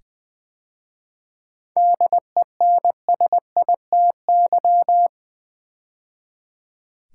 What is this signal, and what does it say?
Morse code, heard as a keyed tone, about 20 wpm, 690 Hz. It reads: DENSITY
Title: 7 Letter Words - First 3 Letters Given - 20wpm